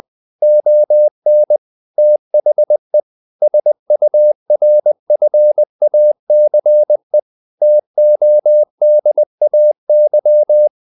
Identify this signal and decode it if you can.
Morse code, heard as a keyed tone, about 20 wpm, 600 Hz. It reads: ON THE SURFACE TODAY
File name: the old man and the sea / the old man and the sea part 1 20wpm